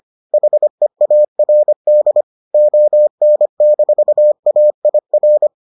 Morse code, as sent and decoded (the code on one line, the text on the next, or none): .... .- .-. -.. / --- -. -....- .- .. .-.
HEARD ON-AIR